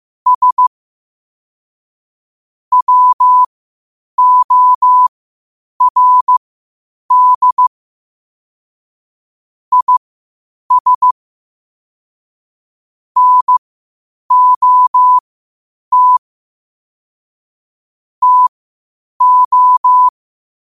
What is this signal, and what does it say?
Morse code, heard as a keyed tone, about 15 wpm, spaced as for 8 wpm, 1000 Hz. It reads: S WORD IS NOT TO